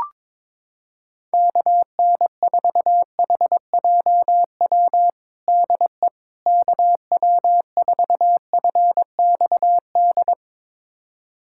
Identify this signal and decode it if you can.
Morse code, heard as a keyed tone, about 22 wpm, 700 Hz. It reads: KN4HJW DE KW4FXD
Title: USA Callsign DE Callsign - 6 Characters - 22wpm